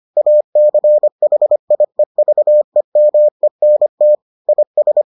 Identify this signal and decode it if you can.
Morse code, heard as a keyed tone, about 25 wpm, 600 Hz. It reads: ACHIEVEMENT IS